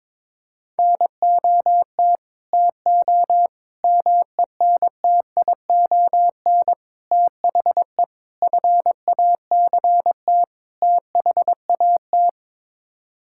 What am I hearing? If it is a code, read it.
Morse code, heard as a keyed tone, about 22 wpm, 700 Hz. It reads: NOT TO MENTION THE FACT THAT
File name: practice / 22wpm / Sets of 6 Words - 22wpm